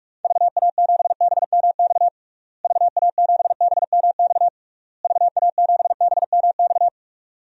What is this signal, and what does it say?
Morse code, heard as a keyed tone, about 41 wpm, 700 Hz. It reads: VA7BMX VA7BMX VA7BMX
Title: DXCC-100 Call Signs - 6 Characters - Repeated 3x - 45wpm